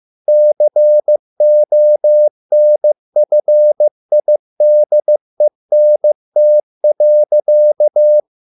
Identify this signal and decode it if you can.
Morse code, heard as a keyed tone, about 15 wpm, 600 Hz. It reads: CONFIDENT.